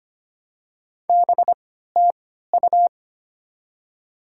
Morse code, heard as a keyed tone, about 25 wpm, spaced as for 14 wpm, 700 Hz. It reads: BTU
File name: Top 20 QSO Elements - 3x Character Spacing - 25wpm